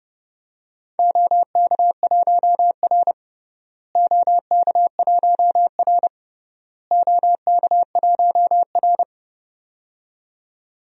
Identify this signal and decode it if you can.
Morse code, heard as a keyed tone, about 32 wpm, 700 Hz. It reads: OK1R OK1R OK1R
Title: DXCC-100 Call Signs - 4 Characters - Repeated 3x - 30wpm